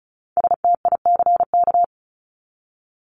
Morse code, heard as a keyed tone, about 34 wpm, 700 Hz. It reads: STICK
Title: Top 1001-1250 Words 35wpm